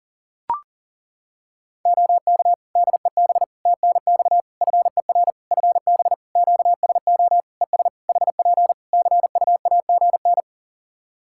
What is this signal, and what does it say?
Morse code, heard as a keyed tone, about 37 wpm, 700 Hz. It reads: OK DEB TNX FER FB QSO ES HP CUAGN
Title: Practice QSO - Exchange 4 - Ending - 40wpm